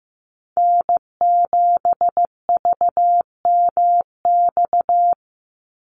Morse code, heard as a keyed tone, about 15 wpm, 700 Hz. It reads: N7VMX
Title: USA Call Signs 15wpm